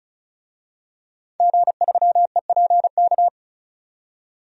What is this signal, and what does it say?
Morse code, heard as a keyed tone, about 36 wpm, 700 Hz. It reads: G3EPK
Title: DXCC-100 Call Signs - 5 Characters - 35wpm